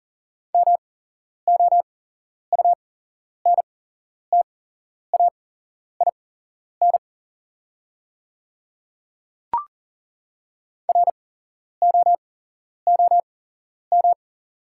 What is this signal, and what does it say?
Morse code, heard as a keyed tone, about 40 wpm, spaced as for 11 wpm, 690 Hz. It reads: MOUNTAIN ROOM
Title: 8x Character Spacing - Top 300 Words - 40wpm